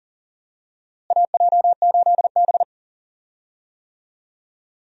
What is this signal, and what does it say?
Morse code, heard as a keyed tone, about 40 wpm, 700 Hz. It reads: AJ8B
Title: USA Call Signs - 4 Characters - Repeated 2x - 40wpm